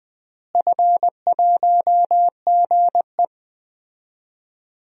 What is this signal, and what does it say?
Morse code, heard as a keyed tone, about 20 wpm, 700 Hz. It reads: F1GE
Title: DXCC-100 Call Signs - 4 Characters - Repeated 3x - 20wpm